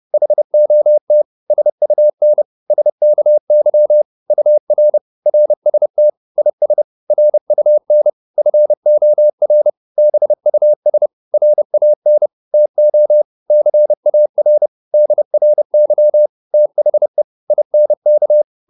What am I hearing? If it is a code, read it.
Morse code, heard as a keyed tone, about 30 wpm, 600 Hz. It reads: HOT SUN SKY UR RST IS RUN FOR BUS RAN TO CAR DRY THE INK